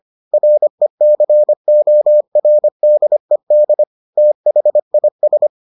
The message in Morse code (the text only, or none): RECORDED THIS